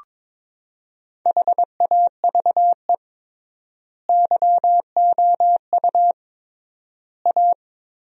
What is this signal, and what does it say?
Morse code, heard as a keyed tone, about 22 wpm, 700 Hz. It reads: HAVE YOU A
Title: 3x Word-Spacing - Sentences from Top 400 Words - 22wpm